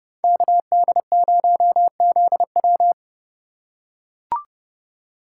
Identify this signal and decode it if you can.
Morse code, heard as a keyed tone, about 28 wpm, 700 Hz. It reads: KD0ZW